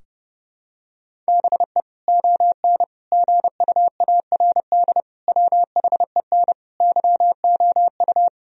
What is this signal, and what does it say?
Morse code, heard as a keyed tone, about 32 wpm, 700 Hz. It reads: BE ON GUARD WHEN YOU
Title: Sentences from Top 1000 Words - Review Part 4 - 30wpm